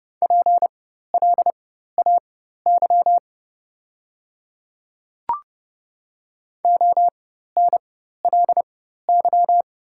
Morse code, heard as a keyed tone, about 30 wpm, spaced as for 14 wpm, 700 Hz. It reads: PLAY ONLY
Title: 4x Character Spacing - Top 200 Words - 30wpm